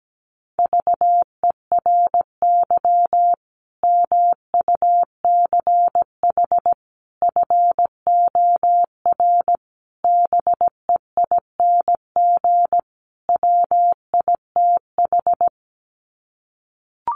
Morse code, heard as a keyed tone, about 17 wpm, 700 Hz. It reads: VERY MUCH FOR BEING WITH
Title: Sets of 5 Words 17wpm